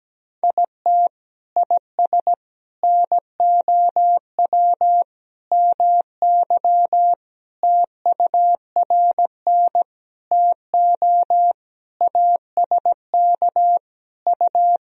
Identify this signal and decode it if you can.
Morse code, heard as a keyed tone, about 17 wpm, 700 Hz. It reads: IT IS NOW MY TURN TO ASK U
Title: Sentences from Top 200 Words and Abbreviations 17wpm